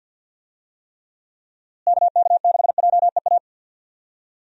Morse code, heard as a keyed tone, about 46 wpm, 700 Hz. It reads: KK6JEA
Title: USA Call Signs - 6 Characters - Repeated 2x - 50wpm